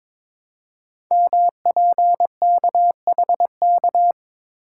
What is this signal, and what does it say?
Morse code, heard as a keyed tone, about 22 wpm, 700 Hz. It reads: MPKHK
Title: Code Groups - 5 Letter:Numbers - 22wpm